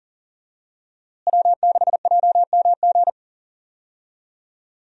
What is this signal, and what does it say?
Morse code, heard as a keyed tone, about 37 wpm, 700 Hz. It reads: W6JMG
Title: USA Call Signs - 5 Characters - Repeated 2x - 40wpm